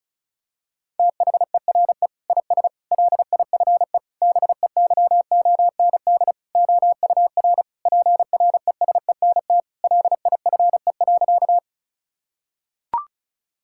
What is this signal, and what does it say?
Morse code, heard as a keyed tone, about 36 wpm, 700 Hz. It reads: THERE IS LIFE BEYOND OUR PRESENT LIFE.
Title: Sentences from Top 1000 Words - Review Part 6 - 35wpm